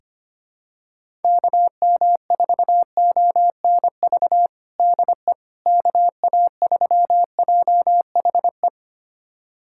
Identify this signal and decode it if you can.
Morse code, heard as a keyed tone, about 25 wpm, 700 Hz. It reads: KM4ONV DE KA3JHE
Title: USA Callsign DE Callsign - 6 Characters - 25wpm